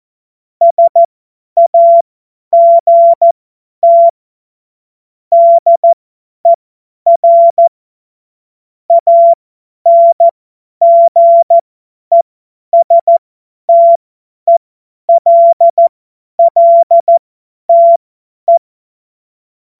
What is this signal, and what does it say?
Morse code, heard as a keyed tone, about 14 wpm, spaced as for 10 wpm, 680 Hz. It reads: SAGT DER ANGESTELLTE
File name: Morsekurs / Witze / Witz 20. Dezember 2022 50 F70